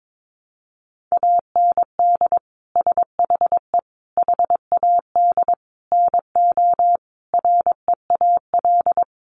Morse code, heard as a keyed tone, about 22 wpm, 700 Hz. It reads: AND SHE HAD NO REAL
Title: Sentences from Top 1200 Words 22wpm